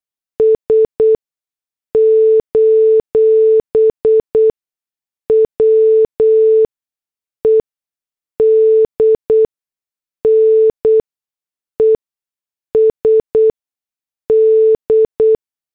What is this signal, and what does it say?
Morse code, heard as a keyed tone, about 8 wpm, spaced as for 6 wpm, 440 Hz. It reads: S: WEDNESD
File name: morse / headlines